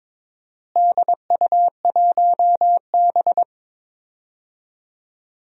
Morse code, heard as a keyed tone, about 22 wpm, 700 Hz. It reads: DU1B
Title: DXCC-100 Call Signs - 4 Characters - Repeated 3x - 22wpm